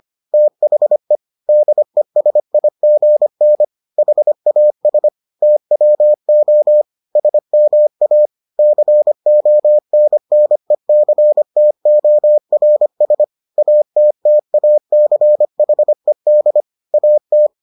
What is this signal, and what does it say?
Morse code, heard as a keyed tone, about 25 wpm, 600 Hz. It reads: THE DESIGN HAS TWO SMA CONNECTORS ATTACHED AT